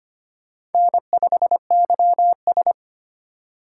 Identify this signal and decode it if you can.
Morse code, heard as a keyed tone, about 31 wpm, 700 Hz. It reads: N5YS
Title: Speed Racing - USA Call Signs 25wpm